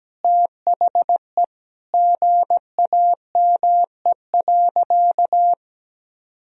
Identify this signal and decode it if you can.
Morse code, heard as a keyed tone, about 17 wpm, 700 Hz. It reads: THE GAME.
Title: Sentences from Top 1000 Words - Review Part 4 - 17wpm